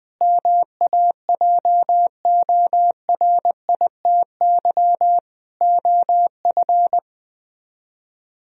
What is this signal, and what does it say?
Morse code, heard as a keyed tone, about 20 wpm, 700 Hz. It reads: MAJORITY OF
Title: Speed-Racing - Sets of 3 Words 17wpm